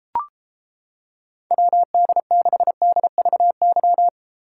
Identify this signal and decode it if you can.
Morse code, heard as a keyed tone, about 31 wpm, 700 Hz. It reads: WD6DVY